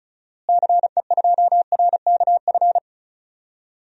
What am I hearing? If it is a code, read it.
Morse code, heard as a keyed tone, about 34 wpm, 700 Hz. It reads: CE2RKF
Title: DXCC-100 Call Signs - 6 Characters - Repeated 2x - 35wpm